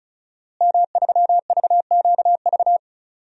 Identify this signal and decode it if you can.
Morse code, heard as a keyed tone, about 34 wpm, 700 Hz. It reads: M3VQV
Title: DXCC-100 Call Signs - 5 Characters - Repeated 3x - 35wpm